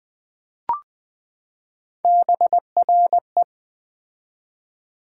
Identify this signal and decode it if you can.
Morse code, heard as a keyed tone, about 20 wpm, 700 Hz. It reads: BRE
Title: Build A Word - 3 Letters At A Time - 9-Letter Words - 20wpm